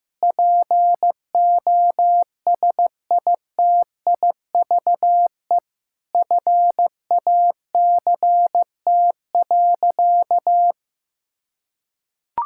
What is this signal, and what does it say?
Morse code, heard as a keyed tone, about 16 wpm, 700 Hz. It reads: POSITIVE FACT.
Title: Sentences from Top 1000 Words - Review Part 5 - 15wpm